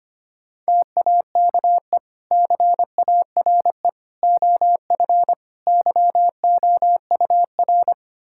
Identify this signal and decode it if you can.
Morse code, heard as a keyed tone, about 25 wpm, 700 Hz. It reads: TAKE CARE OF YOUR
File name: practice / 25wpm / Sentences with 2-Word Phrases - 25wpm